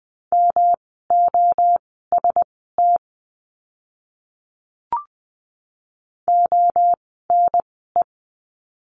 Morse code, heard as a keyed tone, about 20 wpm, spaced as for 15 wpm, 700 Hz. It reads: MOST ONE
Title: 139 Lesson 23f - Letter f - Words 20wpm